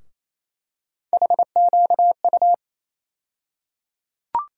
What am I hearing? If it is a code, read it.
Morse code, heard as a keyed tone, about 28 wpm, 700 Hz. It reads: HQU
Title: USA Call Sign Postfixes - 3 Letters 28wpm